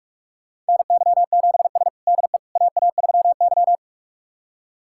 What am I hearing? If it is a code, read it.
Morse code, heard as a keyed tone, about 42 wpm, 700 Hz. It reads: NY7S DE AA3Y
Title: USA Callsign DE Callsign - 4 Characters - 45wpm